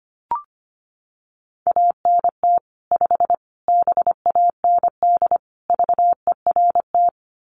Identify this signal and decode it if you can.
Morse code, heard as a keyed tone, about 25 wpm, 700 Hz. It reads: ANT 5 BAND VERT
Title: Common QSO Phrases 25wpm